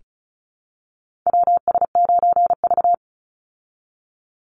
Morse code, heard as a keyed tone, about 34 wpm, 700 Hz. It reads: WS9V
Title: Call Signs Repeated 2x - 35wpm